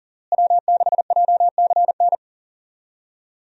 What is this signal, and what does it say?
Morse code, heard as a keyed tone, about 40 wpm, 700 Hz. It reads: W6JCN
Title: USA Call Signs - 5 Characters - Repeated 2x - 40wpm